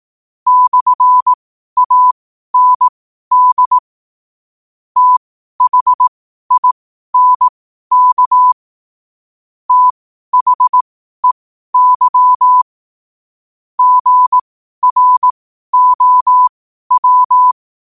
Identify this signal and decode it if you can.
Morse code, heard as a keyed tone, about 18 wpm, spaced as for 12 wpm, 1000 Hz. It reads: /AND THINK THEY GROW
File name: qotdfiles / 12wpm